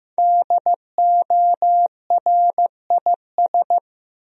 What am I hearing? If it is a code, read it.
Morse code, heard as a keyed tone, about 15 wpm, 700 Hz. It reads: DORIS
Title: USA Common Names - Max 5 Letters - 15wpm